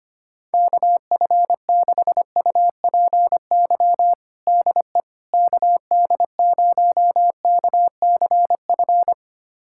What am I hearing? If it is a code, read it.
Morse code, heard as a keyed tone, about 25 wpm, 700 Hz. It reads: KF6UPY DE KD0KCF
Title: USA Callsign DE Callsign - 6 Characters - 25wpm